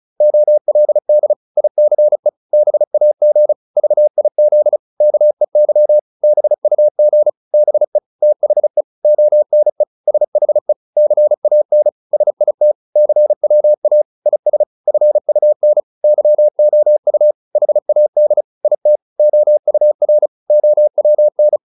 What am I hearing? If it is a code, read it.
Morse code, heard as a keyed tone, about 34 wpm, 600 Hz. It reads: OLD ICE BAG VIZ KEY BUG BE THE ONE SHE CAN SIT CWA IS FUN YOU HAD IT OUR OWN